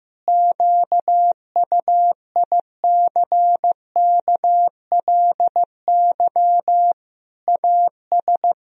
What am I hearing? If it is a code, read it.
Morse code, heard as a keyed tone, about 16 wpm, 700 Hz. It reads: QUICKLY AS
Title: Sentences from Top 1000 Words - Review Part 4 - 15wpm